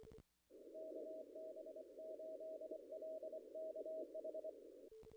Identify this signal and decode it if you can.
Morse code, heard as a keyed tone, about 24 wpm, 630 Hz, with noise. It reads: KB8LKH
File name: contests / ss-cw-k5tr-op-n5rz -0000